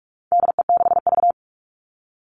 Morse code, heard as a keyed tone, about 44 wpm, 700 Hz. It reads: DE6V
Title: All Cut Numbers - 4-Digit 45wpm